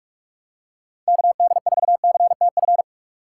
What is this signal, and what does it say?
Morse code, heard as a keyed tone, about 44 wpm, 700 Hz. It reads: KD4CTF